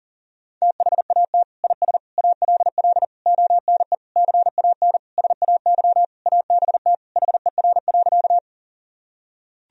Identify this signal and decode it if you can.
Morse code, heard as a keyed tone, about 37 wpm, 700 Hz. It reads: THAT IS ALL ONE CAN SAY ABT HER.
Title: Sentences from Top 200 Words and Abbreviations 40wpm